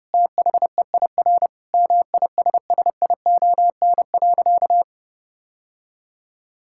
Morse code, heard as a keyed tone, about 30 wpm, 700 Hz. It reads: THEIR MISSION.
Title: Sentences from Top 1600 Words 30wpm